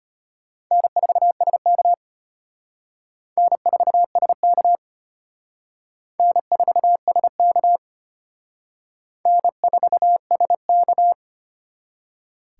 Morse code, heard as a keyed tone, about 34 wpm, 700 Hz. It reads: N4SK N4SK N4SK N4SK